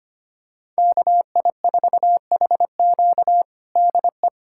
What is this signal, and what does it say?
Morse code, heard as a keyed tone, about 25 wpm, 700 Hz. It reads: KI4HQ DE
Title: USA Callsign DE Callsign 25wpm